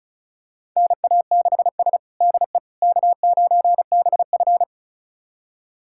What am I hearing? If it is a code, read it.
Morse code, heard as a keyed tone, about 36 wpm, 700 Hz. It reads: NA6S DE K9BF
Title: USA Callsign DE Callsign - 4 Characters - 35wpm